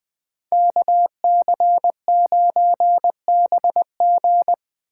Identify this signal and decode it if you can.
Morse code, heard as a keyed tone, about 21 wpm, 700 Hz. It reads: KC9BG
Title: K1USN SST QSO Practice - 20wpm